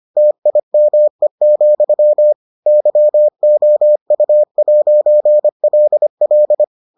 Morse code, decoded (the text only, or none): TIME, YOU'LL